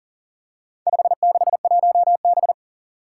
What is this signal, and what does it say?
Morse code, heard as a keyed tone, about 40 wpm, 700 Hz. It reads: 561B
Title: Four Challenging Letter-Number Combinations 40wpm